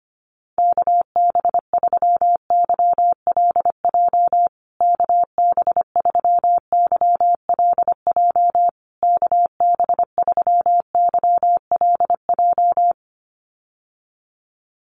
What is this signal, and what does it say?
Morse code, heard as a keyed tone, about 25 wpm, 700 Hz. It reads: KB3YLJ KB3YLJ KB3YLJ
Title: USA Call Signs Repeated 3x 25wpm